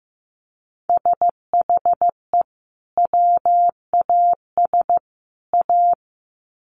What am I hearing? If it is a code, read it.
Morse code, heard as a keyed tone, about 15 wpm, 700 Hz. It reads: SHE WAS A
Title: Sets of 3 Words 15wpm